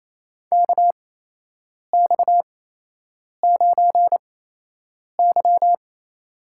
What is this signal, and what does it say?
Morse code, heard as a keyed tone, about 28 wpm, spaced as for 8 wpm, 700 Hz. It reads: KX9Y
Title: USA Call Signs - 4 Characters - 8x Character Spacing - 28wpm